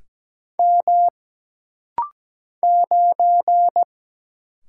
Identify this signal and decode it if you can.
Morse code, heard as a keyed tone, about 17 wpm, 700 Hz. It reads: M 9
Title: Single Letter-Number - Rapid-Fire 17wpm